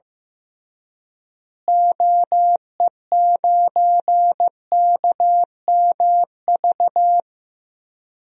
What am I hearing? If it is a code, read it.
Morse code, heard as a keyed tone, about 15 wpm, 700 Hz. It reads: OE9KMV
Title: DXCC-100 Call Signs - 6 Characters - Repeated 3x - 15wpm